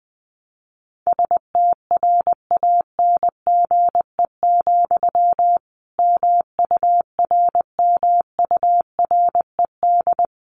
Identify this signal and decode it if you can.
Morse code, heard as a keyed tone, about 20 wpm, 700 Hz. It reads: STRANGE, MURMURED
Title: A Sound Of Thunder 20wpm